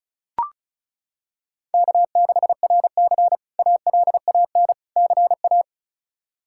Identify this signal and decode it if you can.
Morse code, heard as a keyed tone, about 34 wpm, 700 Hz. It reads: K6RC ALAN CA
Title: K1USN SST QSO Practice - 35wpm